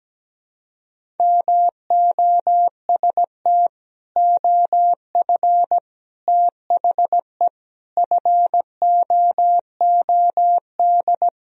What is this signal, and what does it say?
Morse code, heard as a keyed tone, about 17 wpm, 700 Hz. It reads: MOST OF THE FOOD
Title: Sentences from Top 1000 Words - Review Part 1 - 17wpm